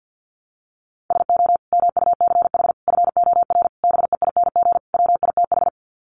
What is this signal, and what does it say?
Morse code, heard as a keyed tone, about 50 wpm, 700 Hz. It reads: SO MUCH FOR BEING WITH
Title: Sets of 5 Words Encore 50wpm